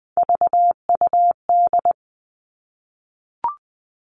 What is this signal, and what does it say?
Morse code, heard as a keyed tone, about 20 wpm, 700 Hz. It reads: VUD
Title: Three Challenging Letter-Number Combinations 20wpm